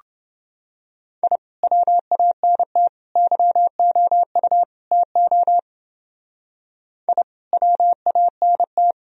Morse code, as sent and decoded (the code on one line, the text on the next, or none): .. / .-- .- -. - / -.-- --- ..- / - --- / .. / .-- .- -. -
I WANT YOU TO I WANT